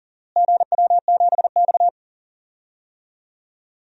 Morse code, heard as a keyed tone, about 40 wpm, 700 Hz. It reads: GW7X
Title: DXCC-100 Call Signs - 4 Characters - Repeated 3x - 40wpm